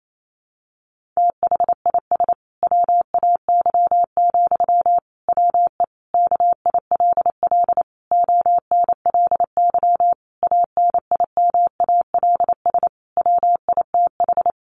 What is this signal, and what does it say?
Morse code, heard as a keyed tone, about 28 wpm, 700 Hz. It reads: THIS WAY, WE KILL ONLY ANIMALS WITH
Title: A Sound Of Thunder 28wpm